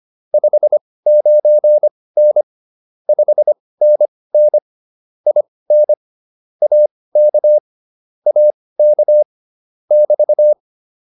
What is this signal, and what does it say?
Morse code, heard as a keyed tone, about 25 wpm, spaced as for 18 wpm, 600 Hz. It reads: 59N 5NN IN AK AK <BT>
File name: QSO107 18